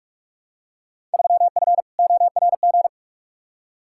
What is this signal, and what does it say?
Morse code, heard as a keyed tone, about 45 wpm, 700 Hz. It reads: 3F ORG
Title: Field Day Exchanges - 45wpm